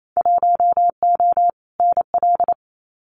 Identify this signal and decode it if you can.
Morse code, heard as a keyed tone, about 28 wpm, 700 Hz. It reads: 1O NL
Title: Winter Field Day Exchanges Encore 28wpm